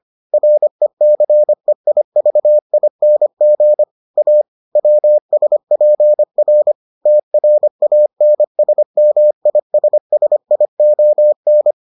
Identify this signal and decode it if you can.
Morse code, heard as a keyed tone, about 25 wpm, 600 Hz. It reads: RECEIVING A WSPR TRANSMISSION